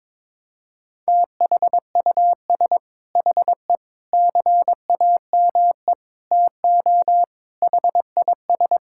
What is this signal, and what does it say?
Morse code, heard as a keyed tone, about 22 wpm, 700 Hz. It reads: THUS HE CAME TO HIS
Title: Sentences from Top 1000 Words - Review Part 5 - 22wpm